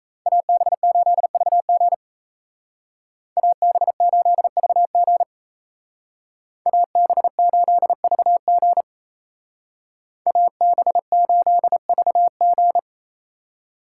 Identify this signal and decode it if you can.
Morse code, heard as a keyed tone, about 36 wpm, 700 Hz. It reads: AB8VG AB8VG AB8VG AB8VG